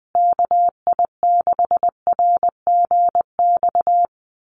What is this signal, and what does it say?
Morse code, heard as a keyed tone, about 20 wpm, 700 Hz. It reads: KI6RGX
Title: USA Call Signs Repeated 2x 20wpm